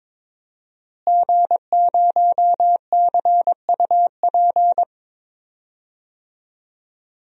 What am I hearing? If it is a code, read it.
Morse code, heard as a keyed tone, about 22 wpm, 700 Hz. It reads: G0CUP
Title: DXCC-100 Call Signs - 5 Characters - Repeated 3x - 22wpm